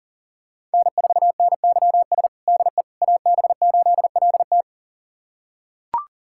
Morse code, heard as a keyed tone, about 37 wpm, 700 Hz. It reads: N4NYS DE AB8LT
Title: USA Callsign DE Callsign - 5 Characters - 40wpm